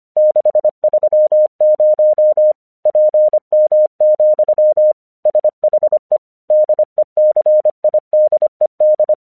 Morse code, heard as a keyed tone, about 25 wpm, 610 Hz. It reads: 630 PM, SHE DECIDED
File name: 1 English 25wpm / AQC18